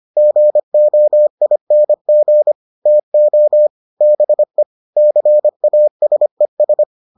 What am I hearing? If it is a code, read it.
Morse code, heard as a keyed tone, about 25 wpm, 600 Hz. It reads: GOING TO BE CASES